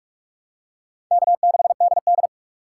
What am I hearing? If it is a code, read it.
Morse code, heard as a keyed tone, about 46 wpm, 700 Hz. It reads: K6DD
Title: USA Call Signs - 4 Characters - 3x Character Spacing - 45wpm